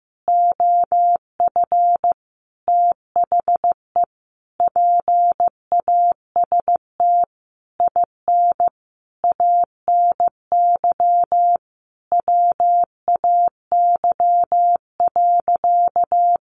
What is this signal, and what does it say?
Morse code, heard as a keyed tone, about 15 wpm, 700 Hz. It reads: OF THE PAST IN ANY WAY.
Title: A Sound Of Thunder 15wpm